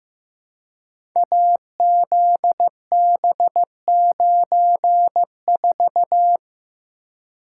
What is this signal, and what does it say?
Morse code, heard as a keyed tone, about 15 wpm, 700 Hz. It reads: AZB94
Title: Code Groups - 5 Letter:Numbers - 15wpm